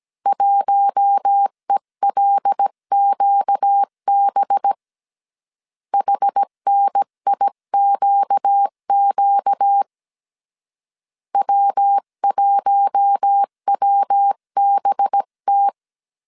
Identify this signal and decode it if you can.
Morse code, heard as a keyed tone, about 17 wpm, 800 Hz. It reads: 1ELQB HNIQQ W1WBT
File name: CW / DET / 1 6 1500C S C 16wpm 2733